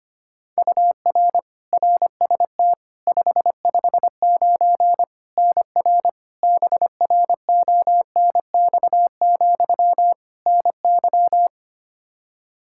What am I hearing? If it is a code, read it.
Morse code, heard as a keyed tone, about 25 wpm, 700 Hz. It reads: UR RST 559 NR BRONX, NY
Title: Practice QSO - Exchange 1 - Beginning - 25wpm